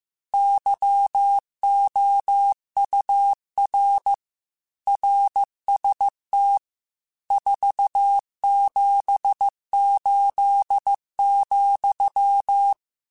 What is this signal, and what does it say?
Morse code, heard as a keyed tone, about 15 wpm, 800 Hz. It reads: YOUR RST 478,